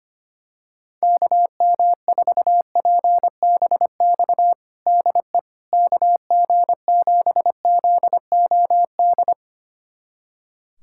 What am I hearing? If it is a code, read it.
Morse code, heard as a keyed tone, about 25 wpm, 700 Hz. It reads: KM4PBX DE KG7ZOD
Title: USA Callsign DE Callsign - 6 Characters - 25wpm